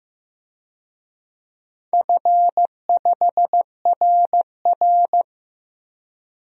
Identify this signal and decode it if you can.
Morse code, heard as a keyed tone, about 15 wpm, 700 Hz. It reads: F5RR
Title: DXCC-100 Call Signs - 4 Characters - Repeated 2x - 15wpm